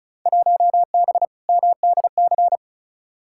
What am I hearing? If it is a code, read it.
Morse code, heard as a keyed tone, about 34 wpm, 700 Hz. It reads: 1B MDC
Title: Field Day Exchanges Encore - 35wpm